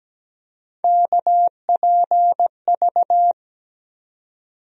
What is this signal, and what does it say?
Morse code, heard as a keyed tone, about 17 wpm, 700 Hz. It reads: KPV
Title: USA Call Sign Postfixes - 3 Letters 17wpm